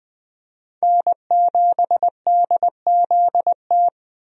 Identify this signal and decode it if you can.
Morse code, heard as a keyed tone, about 20 wpm, 700 Hz. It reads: N7DZT
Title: USA Call Signs - 5 Characters - Repeated 2x - 20wpm